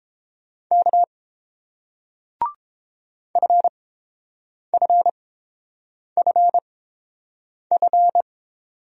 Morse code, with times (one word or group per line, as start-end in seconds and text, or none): 0.7-1.0 K
3.4-8.2 FFFF